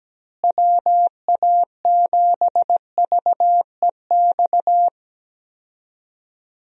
Morse code, heard as a keyed tone, about 17 wpm, 700 Hz. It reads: WA7VEX
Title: USA Call Signs - 6 Characters - Repeated 2x - 17wpm